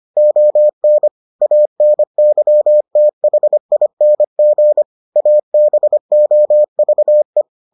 Morse code, heard as a keyed tone, about 25 wpm, 600 Hz. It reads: ON ANYTHING ABOVE